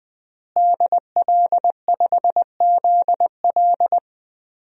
Morse code, heard as a keyed tone, about 20 wpm, 700 Hz. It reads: DL5ZL
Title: DXCC-100 Call Signs - 5 Characters - Repeated 3x - 20wpm